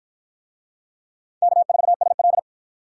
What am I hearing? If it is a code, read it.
Morse code, heard as a keyed tone, about 53 wpm, 700 Hz. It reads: K4SL